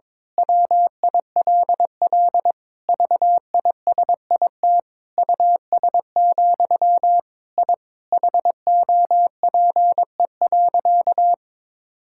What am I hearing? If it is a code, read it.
Morse code, heard as a keyed tone, about 22 wpm, 700 Hz. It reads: WILL VISIT US, I HOPE.
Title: Sentences from Top 1000 Words - Review Part 6 - 22wpm